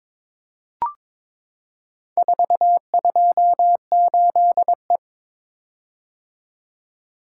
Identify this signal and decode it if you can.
Morse code, heard as a keyed tone, about 22 wpm, 700 Hz. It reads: 428E
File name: practice / 22wpm / Four Letter-Number Combinations - 22wpm